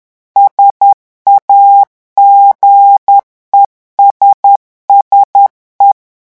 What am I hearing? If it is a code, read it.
Morse code, heard as a keyed tone, about 10 wpm, 800 Hz. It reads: SAGESSE
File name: morse mp3 / revision francais 10 mots minute